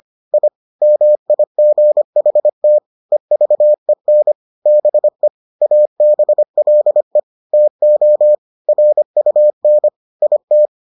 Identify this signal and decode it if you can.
Morse code, heard as a keyed tone, about 25 wpm, 600 Hz. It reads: I MIGHT EVEN BE ABLE TO RUN IT